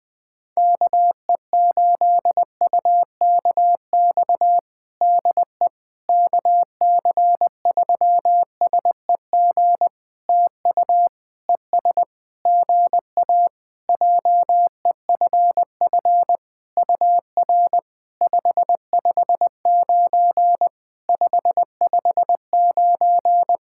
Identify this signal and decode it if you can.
Morse code, heard as a keyed tone, about 20 wpm, 700 Hz. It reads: KE8UKX DE KC3SEG TU ES GA JEFF UR 559 559
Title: POTA QSO Practice - 20wpm